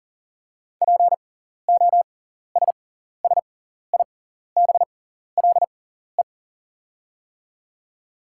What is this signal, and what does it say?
Morse code, heard as a keyed tone, about 38 wpm, spaced as for 14 wpm, 700 Hz. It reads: POSSIBLE
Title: 6x Character Spacing - Top 500 Words - 40wpm